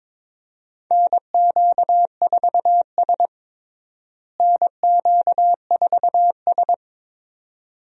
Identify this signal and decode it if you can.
Morse code, heard as a keyed tone, about 22 wpm, 700 Hz. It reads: NQ4S NQ4S
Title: USA Call Signs - 4 Characters - Repeated 2x - 22wpm